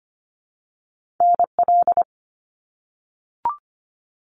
Morse code, heard as a keyed tone, about 25 wpm, 700 Hz. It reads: NL
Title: Canadian Provinces and Territories - Repeated 3x - 25wpm